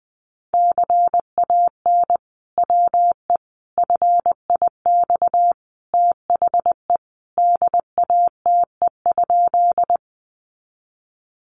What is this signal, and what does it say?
Morse code, heard as a keyed tone, about 19 wpm, 700 Hz. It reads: CAN WE FIX THE DATE?
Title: Sentences from Top 1400 Words 20wpm